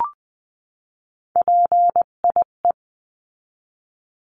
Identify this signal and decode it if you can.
Morse code, heard as a keyed tone, about 20 wpm, 700 Hz. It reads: PIE
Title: 258 Lesson 42c - Words 16-20wpm